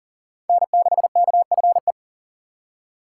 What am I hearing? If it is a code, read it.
Morse code, heard as a keyed tone, about 37 wpm, 700 Hz. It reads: N6KFE
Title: USA Call Signs - 5 Characters - 4x Character Spacing - 40wpm